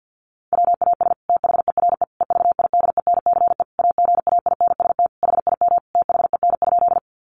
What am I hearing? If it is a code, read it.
Morse code, heard as a keyed tone, about 50 wpm, 700 Hz. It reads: WAS THERE EVIDENCE AGAINST HIM THEN?
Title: Sentences from Top 1300 Words 50wpm